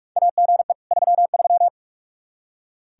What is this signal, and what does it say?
Morse code, heard as a keyed tone, about 46 wpm, 700 Hz. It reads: AGE 33
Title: Practice QSO - Exchange 3 - More Info - 45wpm